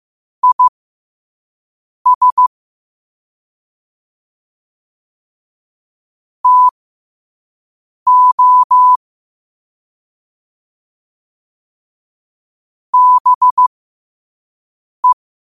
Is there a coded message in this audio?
Yes: Morse code from 0.4 to 15.1 s, heard as a keyed tone, about 15 wpm, spaced as for 5 wpm, 1000 Hz. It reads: IS TO BE